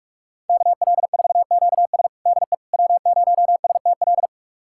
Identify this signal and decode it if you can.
Morse code, heard as a keyed tone, about 46 wpm, 700 Hz. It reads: KL4QS DE W0STL
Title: USA Callsign DE Callsign - 5 Characters - 45wpm